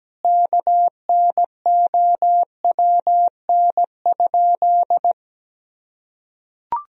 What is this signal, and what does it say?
Morse code, heard as a keyed tone, about 17 wpm, 700 Hz. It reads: KNOWN?
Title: Sentences from Top 1000 Words - Review Part 3 - 17wpm